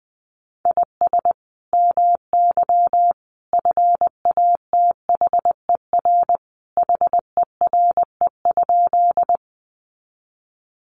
Morse code, heard as a keyed tone, about 19 wpm, 700 Hz. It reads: IS MY FATHER HERE?
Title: Sentences from Top 300 Words 20wpm